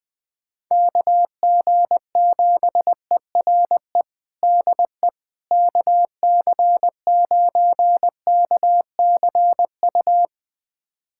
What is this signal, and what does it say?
Morse code, heard as a keyed tone, about 20 wpm, 700 Hz. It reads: KG7ERE DE KC9KCU
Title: USA Callsign DE Callsign 20wpm